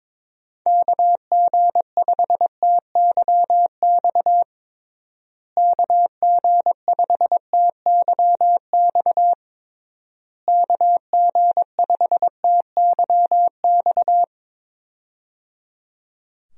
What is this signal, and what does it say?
Morse code, heard as a keyed tone, about 22 wpm, 700 Hz. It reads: KG5TYX KG5TYX KG5TYX
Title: USA Call Signs - 6 Characters - Repeated 3x - 22wpm